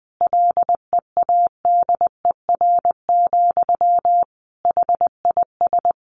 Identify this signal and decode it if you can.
Morse code, heard as a keyed tone, about 20 wpm, 700 Hz. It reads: LEADER, HIS
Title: A Sound Of Thunder 20wpm